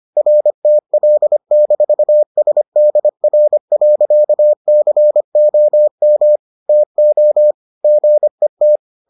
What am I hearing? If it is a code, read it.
Morse code, heard as a keyed tone, about 25 wpm, 600 Hz. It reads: RTL-SDR.COM TO GET